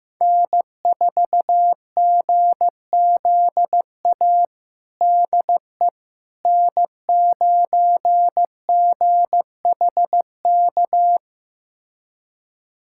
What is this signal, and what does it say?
Morse code, heard as a keyed tone, about 15 wpm, 700 Hz. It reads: N4GZA DE N9GHK
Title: USA Callsign DE Callsign - 5 Characters - 15wpm